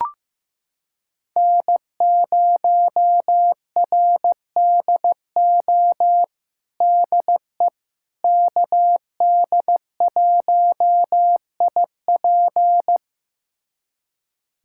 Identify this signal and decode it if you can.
Morse code, heard as a keyed tone, about 16 wpm, 700 Hz. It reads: N0RDO DE KD1IP
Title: USA Callsign DE Callsign - 5 Characters - 15wpm